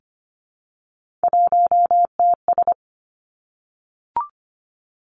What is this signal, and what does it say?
Morse code, heard as a keyed tone, about 25 wpm, 700 Hz. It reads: 1TS
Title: Three Challenging Letter-Number Combinations 25wpm